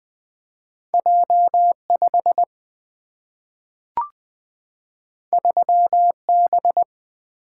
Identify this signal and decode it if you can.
Morse code, heard as a keyed tone, about 20 wpm, 700 Hz. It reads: J5 3B
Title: Single-Letter Single-Number 20wpm